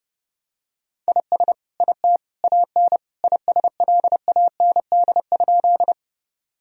Morse code, heard as a keyed tone, about 32 wpm, 700 Hz. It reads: IS IT AN ISLAND?
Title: Sentences from Top 1000 Words - Review Part 3 - 30wpm